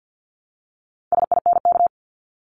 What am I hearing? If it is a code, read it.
Morse code, heard as a keyed tone, about 50 wpm, 700 Hz. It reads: SINK
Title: Top 1751-2000 Words 50wpm